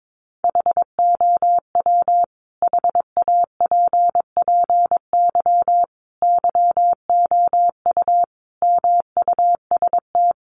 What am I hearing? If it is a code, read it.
Morse code, heard as a keyed tone, about 22 wpm, 700 Hz. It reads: HOW HAPPY YOU MUST
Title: Sentences from Top 1000 Words 22wpm